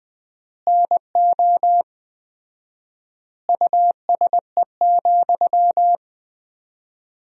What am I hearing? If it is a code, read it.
Morse code, heard as a keyed tone, about 20 wpm, 700 Hz. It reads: NO USE,
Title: 4x Word-Spacing - Sentences from Top 500 Words - 20wpm